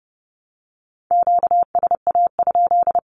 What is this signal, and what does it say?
Morse code, heard as a keyed tone, about 30 wpm, 700 Hz. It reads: QSA?
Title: Q-Codes 30wpm